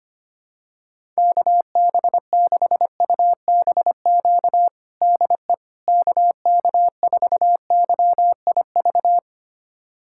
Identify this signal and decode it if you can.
Morse code, heard as a keyed tone, about 25 wpm, 700 Hz. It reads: KB6UBQ DE KK4YIV
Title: USA Callsign DE Callsign - 6 Characters - 25wpm